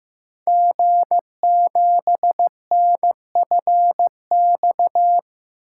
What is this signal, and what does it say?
Morse code, heard as a keyed tone, about 15 wpm, 700 Hz. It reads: G7NFX
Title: Code Groups - 5 Letter:Numbers - 15wpm